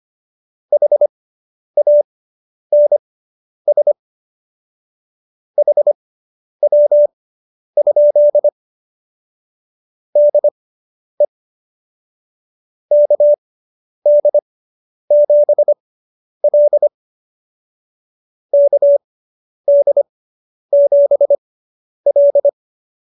Morse code, heard as a keyed tone, about 25 wpm, spaced as for 10 wpm, 590 Hz. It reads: HANS HW? DE KD7L KD7L